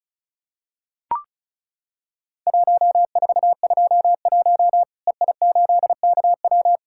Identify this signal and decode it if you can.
Morse code, heard as a keyed tone, about 36 wpm, 700 Hz. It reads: 1421 EI8KW
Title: Worked All Europe DX Contest - QTC Exchange - 35wpm